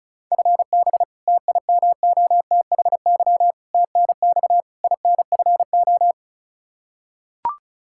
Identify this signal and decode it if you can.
Morse code, heard as a keyed tone, about 36 wpm, 700 Hz. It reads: FB TIMOTHY TNX INFO
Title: K1USN SST QSO Practice 35wpm